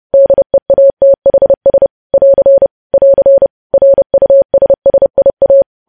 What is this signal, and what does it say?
Morse code, heard as a keyed tone, about 30 wpm, 550 Hz. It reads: DEATHS <AR> <AR> RUSSIA
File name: podcast / News headlines-1x-30x30